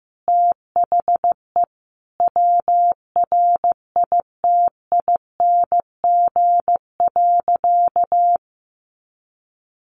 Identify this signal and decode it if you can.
Morse code, heard as a keyed tone, about 15 wpm, 700 Hz. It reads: THE WRITING.
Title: Sentences from Top 1400 Words 15wpm